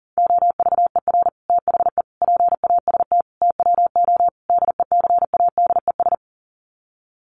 Morse code, heard as a keyed tone, about 40 wpm, 700 Hz. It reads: OVER THE PAST TWO DECADES
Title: Sets of 5 Words Encore2 40wpm